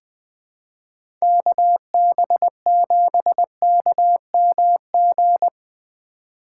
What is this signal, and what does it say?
Morse code, heard as a keyed tone, about 20 wpm, 700 Hz. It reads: KB7KMG
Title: USA Call Signs - 6 Characters - 4x Character Spacing - 20wpm